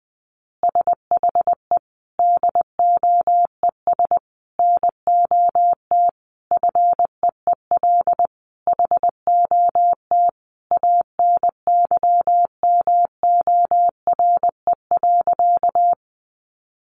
Morse code, heard as a keyed tone, about 20 wpm, 700 Hz. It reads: SHE DOES NOT FEEL HOT ANYMORE.
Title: Easy Phrases 20wpm